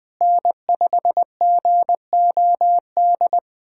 Morse code, heard as a keyed tone, about 20 wpm, 700 Hz. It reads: N5GOD
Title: USA Callsign DE Callsign - 5 Characters - 20wpm